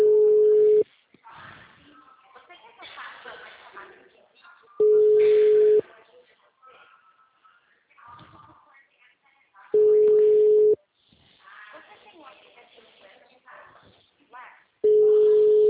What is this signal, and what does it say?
Ringback tone (Europe, 425 Hz), with speech.